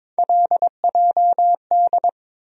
Morse code, heard as a keyed tone, about 22 wpm, 700 Hz. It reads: LJD
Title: USA Call Sign Postfixes - 3 Letters 22wpm